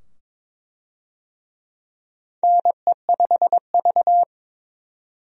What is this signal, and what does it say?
Morse code, heard as a keyed tone, about 22 wpm, 700 Hz. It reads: NE5V
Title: USA Call Signs - 4 Characters - 22wpm